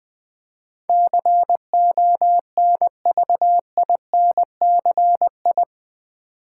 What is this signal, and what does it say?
Morse code, heard as a keyed tone, about 20 wpm, 700 Hz. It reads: CONVINCI
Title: Build A Word - 4 Letters At A Time - 10-Letter Words - 20wpm